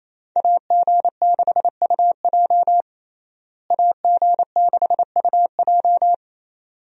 Morse code, heard as a keyed tone, about 28 wpm, 700 Hz. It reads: AG6UJ AG6UJ
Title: USA Call Signs - 5 Characters - Repeated 2x - 28wpm